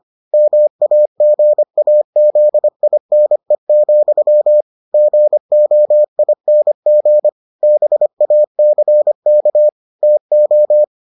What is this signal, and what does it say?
Morse code, heard as a keyed tone, about 25 wpm, 600 Hz. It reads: MAGAZINE, GOING BACK TO